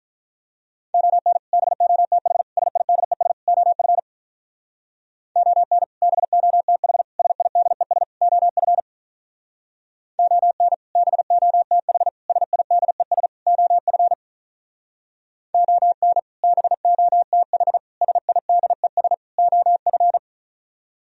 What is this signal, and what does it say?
Morse code, heard as a keyed tone, about 45 wpm, 700 Hz. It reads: ON BOTH SIDES OF ON BOTH SIDES OF ON BOTH SIDES OF ON BOTH SIDES OF